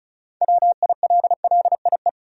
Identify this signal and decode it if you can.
Morse code, heard as a keyed tone, about 34 wpm, 700 Hz. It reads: WILLIE
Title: USA Common Names - 35wpm